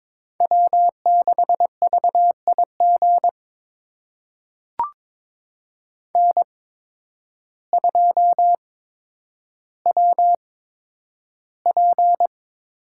Morse code, heard as a keyed tone, about 22 wpm, spaced as for 10 wpm, 700 Hz. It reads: W6VIG N2WP
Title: USA Call Signs - 8x Character Spacing - 22wpm